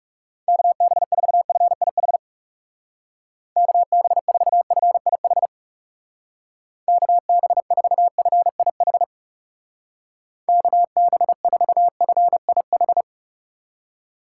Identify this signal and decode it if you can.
Morse code, heard as a keyed tone, about 36 wpm, 700 Hz. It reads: KB4FIH KB4FIH KB4FIH KB4FIH
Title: Speed Racing - USA 2x3 Call Signs 30wpm